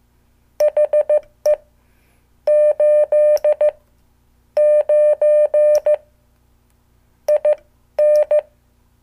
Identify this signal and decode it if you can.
Morse code, heard as a keyed tone, about 15 wpm, spaced as for 12 wpm, 600 Hz. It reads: HE 8 9 IN
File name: morse / qso